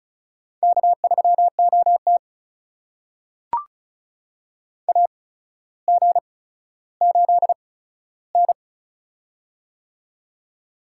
Morse code, heard as a keyed tone, about 36 wpm, spaced as for 15 wpm, 700 Hz. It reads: K3OT AG8N
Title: USA Call Signs - 4 Characters - 8x Character Spacing - 35wpm